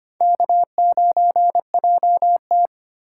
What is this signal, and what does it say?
Morse code, heard as a keyed tone, about 31 wpm, 700 Hz. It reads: K9JT
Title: Speed Racing - USA Call Signs 25wpm